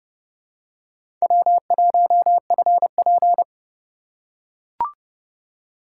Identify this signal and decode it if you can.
Morse code, heard as a keyed tone, about 31 wpm, 700 Hz. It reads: W1FP
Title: USA Call Signs - 4 Characters - 4x Character Spacing - 30wpm